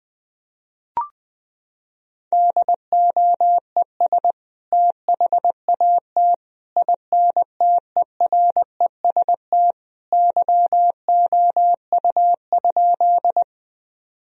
Morse code, heard as a keyed tone, about 20 wpm, 700 Hz. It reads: DOES THAT INTEREST YOU?
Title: Sentences from Top 1000 Words - Review Part 2 - 20wpm